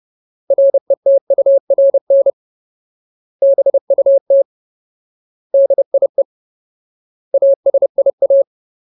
Morse code, heard as a keyed tone, about 30 wpm, 550 Hz. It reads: RETURN BUT DIE ASIA
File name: Ultimate Word List / 30WPM Ultimate